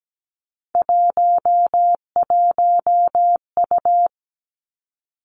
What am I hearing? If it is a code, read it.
Morse code, heard as a keyed tone, about 17 wpm, 700 Hz. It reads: 11U
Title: Three Challenging Letter-Number Combinations 17wpm